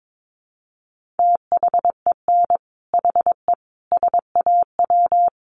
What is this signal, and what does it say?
Morse code, heard as a keyed tone, about 22 wpm, 700 Hz. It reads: THEN HE SAW
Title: Sentences from Top 400 Words 22wpm